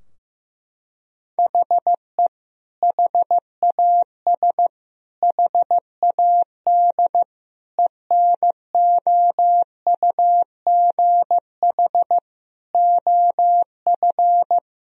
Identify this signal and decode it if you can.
Morse code, heard as a keyed tone, about 15 wpm, 700 Hz. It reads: HE HAS HAD ENOUGH OF
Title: Sentences from Top 1000 Words - Review Part 3 - 15wpm